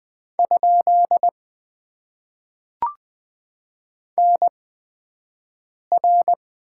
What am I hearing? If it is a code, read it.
Morse code, heard as a keyed tone, about 21 wpm, spaced as for 5 wpm, 700 Hz. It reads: ? NR
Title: Top 90 QSO Elements - 8x Character Spacing - 20wpm